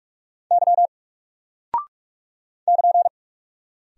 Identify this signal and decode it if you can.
Morse code, heard as a keyed tone, about 44 wpm, 700 Hz. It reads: Y <KN>
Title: Single Character (Letter:Number:Punctuation:Prosign) 45wpm